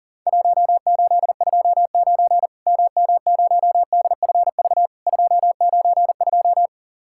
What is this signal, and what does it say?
Morse code, heard as a keyed tone, about 40 wpm, 700 Hz. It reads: 1829 MM0DFV 292
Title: Worked All Europe DX Contest - QTC Exchange - 40wpm